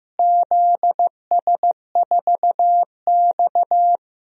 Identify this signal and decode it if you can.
Morse code, heard as a keyed tone, about 15 wpm, 700 Hz. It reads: ZS4X
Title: DXCC-100 Call Signs - 4 Characters - Repeated 2x - 15wpm